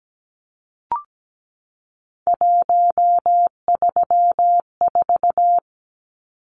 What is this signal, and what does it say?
Morse code, heard as a keyed tone, about 17 wpm, 700 Hz. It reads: 134
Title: Numbers - 3-Digit 17wpm